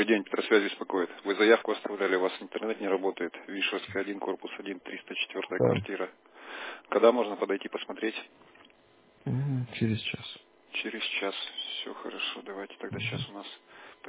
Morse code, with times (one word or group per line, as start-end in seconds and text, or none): none